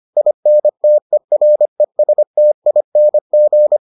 Morse code, heard as a keyed tone, about 25 wpm, 600 Hz. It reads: INTERESTING